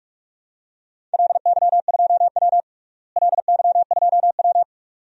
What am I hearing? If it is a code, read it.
Morse code, heard as a keyed tone, about 45 wpm, 700 Hz. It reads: LY2W LY2W
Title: DXCC-100 Call Signs - 4 Characters - Repeated 2x - 45wpm